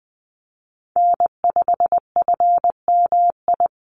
Morse code, heard as a keyed tone, about 20 wpm, 700 Hz. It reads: N5FMI
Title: USA Call Signs Repeated 3x 20wpm